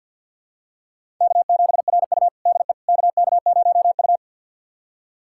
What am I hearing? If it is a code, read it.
Morse code, heard as a keyed tone, about 50 wpm, 700 Hz. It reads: K7RU DE KK0U
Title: USA Callsign DE Callsign - 4 Characters - 50wpm